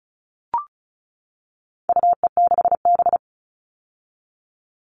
Morse code, heard as a keyed tone, about 34 wpm, 700 Hz. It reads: UE6B
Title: All Cut Numbers - 4-Digit 35wpm